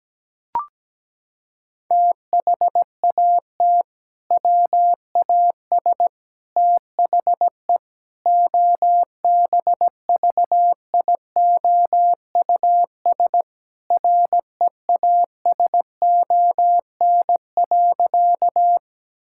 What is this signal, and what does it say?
Morse code, heard as a keyed tone, about 17 wpm, 700 Hz. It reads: THAT WAS THE OBVIOUS REASON.
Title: Sentences from Top 1600 Words 17wpm